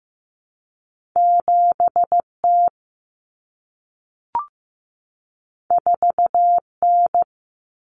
Morse code, heard as a keyed tone, about 15 wpm, 700 Hz. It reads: 7T 4N
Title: Common Cut Numbers - 2-Digit 15wpm